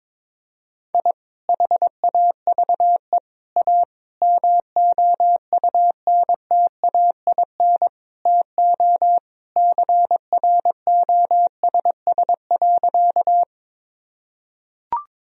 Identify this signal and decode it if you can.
Morse code, heard as a keyed tone, about 22 wpm, 700 Hz. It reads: I HAVE A MOUNTAIN TO CROSS.
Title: Sentences from Top 1000 Words - Review Part 1 - 22wpm